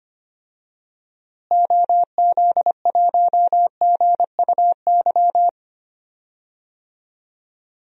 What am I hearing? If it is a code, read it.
Morse code, heard as a keyed tone, about 25 wpm, 700 Hz. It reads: OZ1GUY
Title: DXCC-100 Call Signs - 6 Characters - Repeated 3x - 25wpm